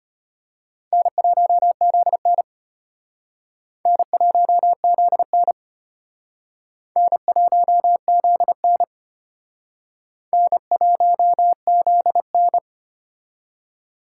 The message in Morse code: N1ZN N1ZN N1ZN N1ZN

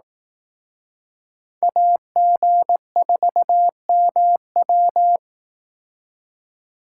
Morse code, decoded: AG4MW

